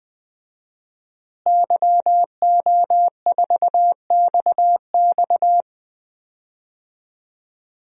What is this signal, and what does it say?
Morse code, heard as a keyed tone, about 21 wpm, 700 Hz. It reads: YO4XX
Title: DXCC-100 Call Signs - 5 Characters - Repeated 3x - 20wpm